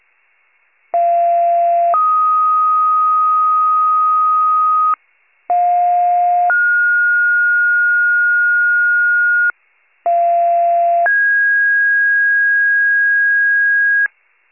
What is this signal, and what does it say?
Two-tone page, heard as a one-second tone, then a three-second tone, about 690 then 1250 Hz, about 710 then 1430 Hz, about 690 then 1650 Hz, with noise. It is heard from 0.9 to 14.1 s.
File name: pager / 46 RESCUE DUTY / wcalert